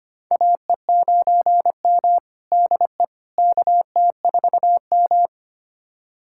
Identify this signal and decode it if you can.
Morse code, heard as a keyed tone, about 25 wpm, 700 Hz. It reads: AE9M DE KT4M
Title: USA Callsign DE Callsign - 4 Characters - 25wpm